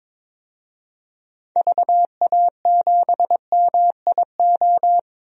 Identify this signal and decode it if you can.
Morse code, heard as a keyed tone, about 21 wpm, 700 Hz. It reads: VA7MIO
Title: DXCC-100 Call Signs - 6 Characters - Repeated 3x - 22wpm